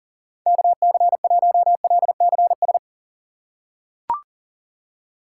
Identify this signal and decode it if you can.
Morse code, heard as a keyed tone, about 38 wpm, 700 Hz. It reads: KC1LCS
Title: USA Call Signs - 6 Characters - 2x Character Spacing - 40wpm